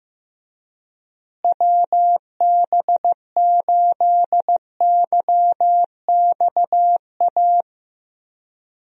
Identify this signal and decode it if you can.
Morse code, heard as a keyed tone, about 15 wpm, 700 Hz. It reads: WB8YXA
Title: USA Call Signs - 6 Characters - Repeated 2x - 15wpm